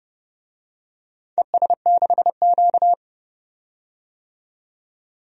Morse code, heard as a keyed tone, about 30 wpm, 700 Hz. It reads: ES6Q
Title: DXCC-100 Call Signs - 4 Characters - 30wpm